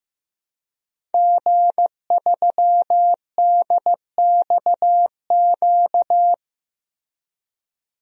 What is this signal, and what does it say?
Morse code, heard as a keyed tone, about 16 wpm, 700 Hz. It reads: G3DXQ
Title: DXCC-100 Call Signs - 5 Characters - Repeated 3x - 15wpm